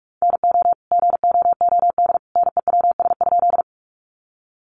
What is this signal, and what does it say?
Morse code, heard as a keyed tone, about 44 wpm, 700 Hz. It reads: NO GOOD NEWS?
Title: Sentences from Top 800 Words 45wpm